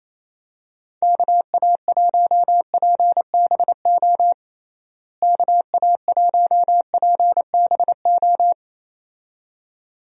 Morse code, heard as a keyed tone, about 28 wpm, 700 Hz. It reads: KA1PBO KA1PBO
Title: USA Call Signs - 6 Characters - Repeated 2x - 28wpm